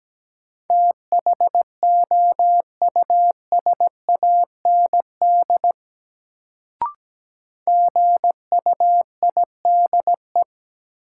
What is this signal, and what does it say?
Morse code, heard as a keyed tone, about 17 wpm, 700 Hz. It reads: THOUSAND GUIDE
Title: Sentences from Top 600 Words - New Word Drill - 17wpm